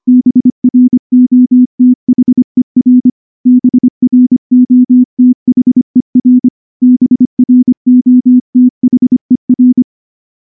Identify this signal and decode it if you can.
Morse code, heard as a keyed tone, about 23 wpm, 250 Hz. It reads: BROTHER BROTHER BROTHER
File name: morse / morse-training-25WPM